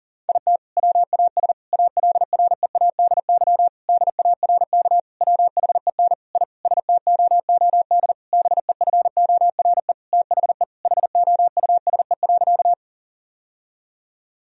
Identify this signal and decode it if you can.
Morse code, heard as a keyed tone, about 40 wpm, 700 Hz. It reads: IT WAS ALREADY DARK WHEN I STOOD BEFORE THE HOUSE.
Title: Sentences from Top 1000 Words - Review Part 2 - 40wpm